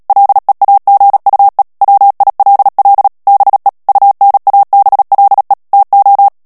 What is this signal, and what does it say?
Morse code, heard as a keyed tone, about 37 wpm, 800 Hz. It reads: LEAGUE WILL BE UNABLE TO
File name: morse mp3 / ANGLAIS texte clair 3 vitesse 35 mots minute